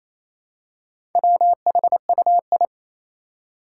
Morse code, heard as a keyed tone, about 27 wpm, 700 Hz. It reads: WHUI
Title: Four Challenging Letter-Number Combinations 28wpm